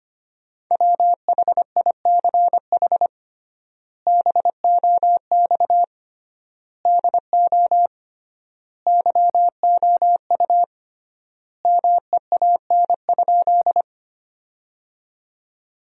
Morse code, heard as a keyed tone, about 25 wpm, 700 Hz. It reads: WHICH BOX DO YOU MEAN?
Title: 3x Word-Spacing - Sentences from Top 500 Words - 25wpm